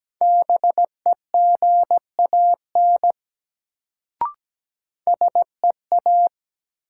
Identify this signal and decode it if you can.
Morse code, heard as a keyed tone, about 17 wpm, 700 Hz. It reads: BEGAN SEA
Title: Sentences from Top 300 Words - New Word Drill - 17wpm